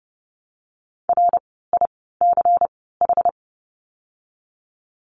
Morse code, heard as a keyed tone, about 30 wpm, 700 Hz. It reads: R I C H
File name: Lesson 16f - Letter c - Words 30wpm